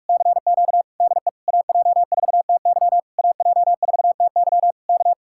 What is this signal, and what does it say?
Morse code, heard as a keyed tone, about 44 wpm, 700 Hz. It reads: CQ DE AJ4TY AJ4TY K